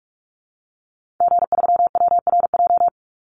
Morse code, heard as a keyed tone, about 44 wpm, 700 Hz. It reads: G3WRJ
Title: Call Signs - 45wpm